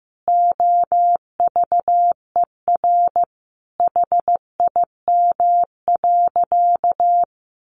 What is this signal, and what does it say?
Morse code, heard as a keyed tone, about 15 wpm, 700 Hz. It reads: OVER HIM.
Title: Additional Easy Phrases 15wpm